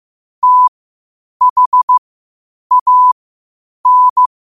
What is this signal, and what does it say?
Morse code, heard as a keyed tone, about 15 wpm, spaced as for 8 wpm, 1000 Hz. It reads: THAN
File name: qotdfiles / 8wpm